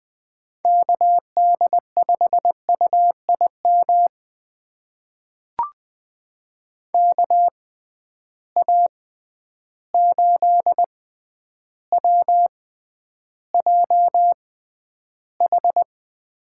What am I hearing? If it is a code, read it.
Morse code, heard as a keyed tone, about 20 wpm, spaced as for 10 wpm, 700 Hz. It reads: KD5UIM KA8WJH
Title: USA Call Signs - 6 Characters - 6x Character Spacing - 20wpm